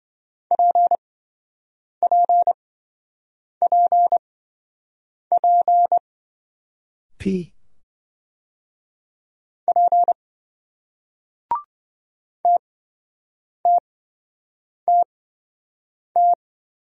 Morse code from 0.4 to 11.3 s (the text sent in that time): PPPP P